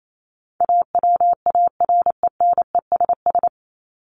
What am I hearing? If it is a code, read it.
Morse code, heard as a keyed tone, about 28 wpm, 700 Hz. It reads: AWARENESS
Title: Top 1501-1750 Words 28wpm